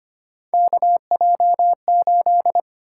Morse code, heard as a keyed tone, about 25 wpm, 700 Hz. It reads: KJ8
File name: Three Letter-Number Combinations - 25wpm